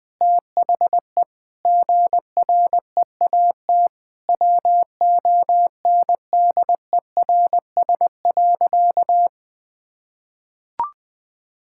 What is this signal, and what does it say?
Morse code, heard as a keyed tone, about 21 wpm, 700 Hz. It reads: THE GREAT WONDERS.
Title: Sentences with 4-Word Phrases - 20wpm